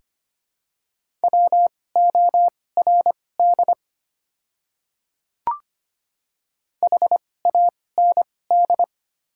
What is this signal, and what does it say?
Morse code, heard as a keyed tone, about 25 wpm, spaced as for 18 wpm, 700 Hz. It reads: WORD HAND
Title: 2x Character Spacing - Top 200 Words - 25wpm